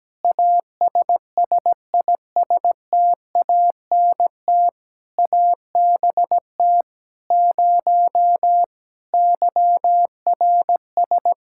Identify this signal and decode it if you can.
Morse code, heard as a keyed tone, about 17 wpm, 700 Hz. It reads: ASSISTANT ABT 0 YRS
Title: K1USN SST QSO Practice 17wpm